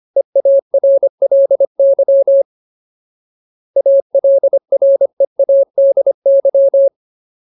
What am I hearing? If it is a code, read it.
Morse code, heard as a keyed tone, about 25 wpm, 550 Hz. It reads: EARLY ALREADY
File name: Ultimate Word List / 25WPM Ultimate